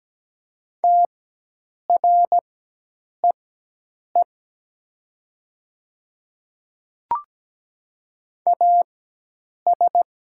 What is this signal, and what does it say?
Morse code, heard as a keyed tone, about 17 wpm, spaced as for 8 wpm, 700 Hz. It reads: TREE AS
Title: 4x Character Spacing - Top 500 Words - 17wpm